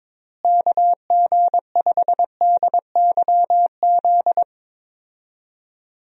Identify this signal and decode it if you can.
Morse code, heard as a keyed tone, about 22 wpm, 700 Hz. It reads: KG5DYZ
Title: USA Call Signs - 6 Characters - Repeated 2x - 22wpm